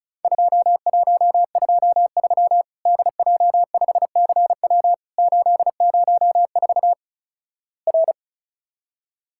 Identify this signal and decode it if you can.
Morse code, heard as a keyed tone, about 36 wpm, 700 Hz. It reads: 2123 DJ5CW 804 R
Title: Worked All Europe DX Contest - QTC Exchange - 35wpm